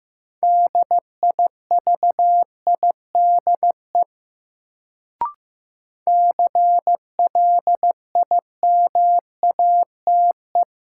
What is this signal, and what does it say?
Morse code, heard as a keyed tone, about 15 wpm, 700 Hz. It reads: DIVIDE CLIMATE
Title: Sentences from Top 1600 Words - New Word Drill Alt - 15wpm